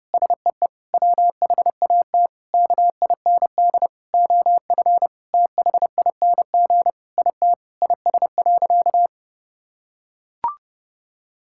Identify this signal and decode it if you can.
Morse code, heard as a keyed tone, about 30 wpm, 700 Hz. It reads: SEE WHAT KIND OF THING IT IS.
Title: Sentences from Top 1000 Words - Review Part 6 - 30wpm